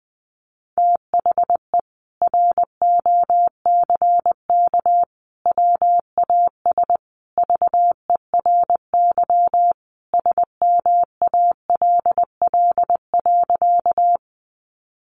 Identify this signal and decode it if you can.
Morse code, heard as a keyed tone, about 20 wpm, 700 Hz. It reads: THE ROCK WAS VERY SMALL.
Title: Sentences from Top 400 Words 20wpm